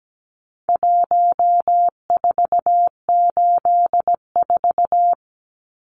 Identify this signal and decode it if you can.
Morse code, heard as a keyed tone, about 17 wpm, 700 Hz. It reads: 1484
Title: Numbers - Contest Style - 4-Digit 17wpm